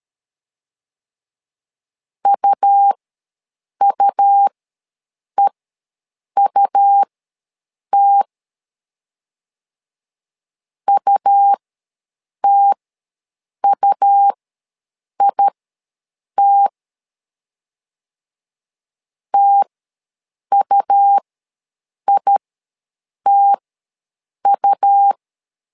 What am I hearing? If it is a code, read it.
Morse code, heard as a keyed tone, about 13 wpm, spaced as for 7 wpm, 800 Hz. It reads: UUEUT UTUIT TUITU
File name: LEZIONI / 2 UHDN / 2 U ws 12wpm 1655